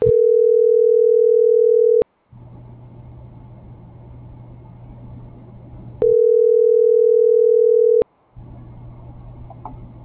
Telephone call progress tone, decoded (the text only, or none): ringback tone